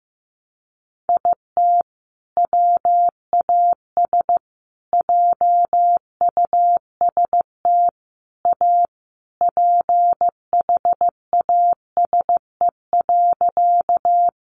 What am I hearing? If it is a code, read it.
Morse code, heard as a keyed tone, about 15 wpm, 700 Hz. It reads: IT WAS JUST A PHASE.
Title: Sentences from Top 1200 Words 15wpm